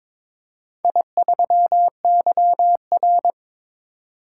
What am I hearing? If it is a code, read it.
Morse code, heard as a keyed tone, about 22 wpm, 700 Hz. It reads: I3YR
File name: practice / 22wpm / Four Letter-Number Combinations - 22wpm